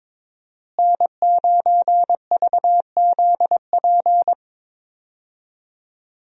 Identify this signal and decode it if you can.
Morse code, heard as a keyed tone, about 22 wpm, 700 Hz. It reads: N9VZP
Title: USA Call Signs - 5 Characters - Repeated 2x - 22wpm